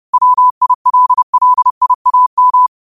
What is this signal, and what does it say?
Morse code, heard as a keyed tone, about 30 wpm, 1000 Hz. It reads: WILLIAM
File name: qotdfiles / 30wpm